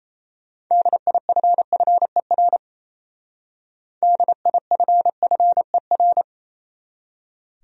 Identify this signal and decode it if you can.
Morse code, heard as a keyed tone, about 32 wpm, 700 Hz. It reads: DIFFER DIFFER
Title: Speed-Racing Top 200 Words 28wpm